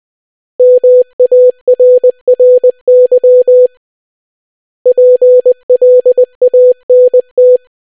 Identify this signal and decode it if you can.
Morse code, heard as a keyed tone, about 20 wpm, 500 Hz. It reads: MARRY PLANT